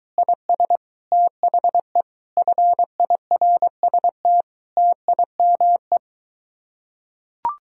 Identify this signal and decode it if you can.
Morse code, heard as a keyed tone, about 22 wpm, 700 Hz. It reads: IS THE FIRST TIME